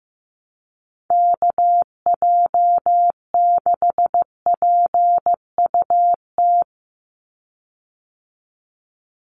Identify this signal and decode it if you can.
Morse code, heard as a keyed tone, about 15 wpm, 700 Hz. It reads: KJ6PUT
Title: USA Call Signs 15wpm